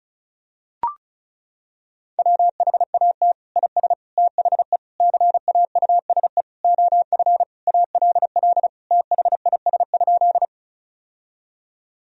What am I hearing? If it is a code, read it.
Morse code, heard as a keyed tone, about 36 wpm, 700 Hz. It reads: WHAT IS THE CAUSE OF ALL THIS?
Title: Sentences from Top 1000 Words - Review Part 1 - 35wpm